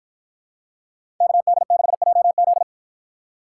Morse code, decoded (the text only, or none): KD6JZ